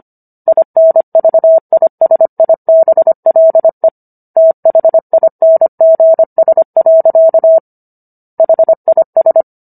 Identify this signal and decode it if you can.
Morse code, heard as a keyed tone, about 25 wpm, 650 Hz. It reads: INVISIBLE THINGS. HIS